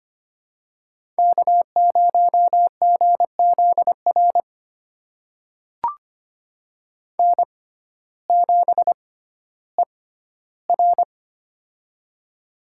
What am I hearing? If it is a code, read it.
Morse code, heard as a keyed tone, about 25 wpm, spaced as for 14 wpm, 700 Hz. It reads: K0GZR N7ER
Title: USA Call Signs - 6x Character Spacing - 25wpm